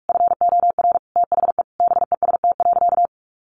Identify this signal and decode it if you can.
Morse code, heard as a keyed tone, about 44 wpm, 700 Hz. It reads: FOR THE BEST.
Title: Sentences from Top 500 Words 45wpm